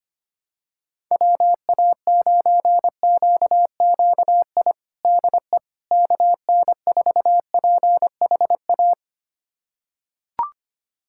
Morse code, heard as a keyed tone, about 25 wpm, 700 Hz. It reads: WA9QQI DE KN4PHA
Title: USA Callsign DE Callsign 25wpm